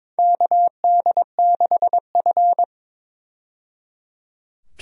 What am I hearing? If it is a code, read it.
Morse code, heard as a keyed tone, about 22 wpm, 700 Hz. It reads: KD6F
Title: USA Call Signs - 4 Characters - 22wpm